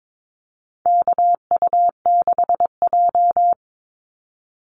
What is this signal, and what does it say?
Morse code, heard as a keyed tone, about 22 wpm, 700 Hz. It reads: KU6J
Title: NA SOTA Activator Callsigns 22wpm